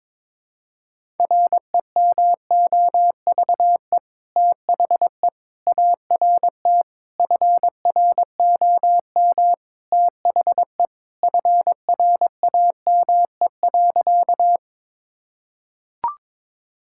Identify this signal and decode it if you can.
Morse code, heard as a keyed tone, about 22 wpm, 700 Hz. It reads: REMOVE THE ART FROM THE FRAME.
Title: Sentences from Top 1000 Words - Review Part 6 - 22wpm